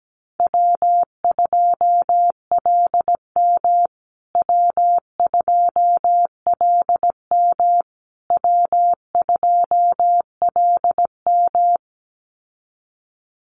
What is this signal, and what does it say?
Morse code, heard as a keyed tone, about 17 wpm, 700 Hz. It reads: W2LM W2LM W2LM
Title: USA Call Signs Repeated 3x 17wpm